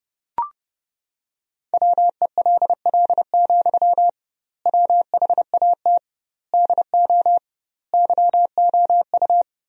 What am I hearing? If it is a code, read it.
Morse code, heard as a keyed tone, about 32 wpm, 700 Hz. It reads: WELL, WHAT DO YOU
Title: 2x Word-Spacing - Sentences from Top 200 Words - 30wpm